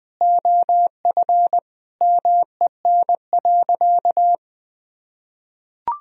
Morse code, heard as a keyed tone, about 20 wpm, 700 Hz. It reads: OF MEN.
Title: Sentences from Top 1000 Words - Review Part 1 - 20wpm